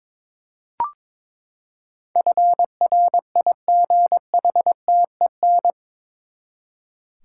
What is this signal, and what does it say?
Morse code, heard as a keyed tone, about 22 wpm, 700 Hz. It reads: FRIGHTEN
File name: practice / 22wpm / Eight Letter Words - 22wpm